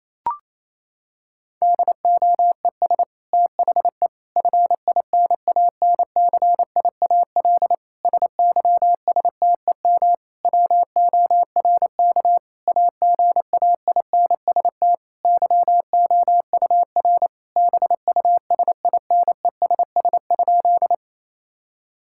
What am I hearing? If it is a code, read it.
Morse code, heard as a keyed tone, about 27 wpm, 700 Hz. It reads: DOES THE FINANCIAL SYSTEM WORK AGAINST YOUR BUSINESS?
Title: Sentences from Top 1000 Words - Review Part 5 - 28wpm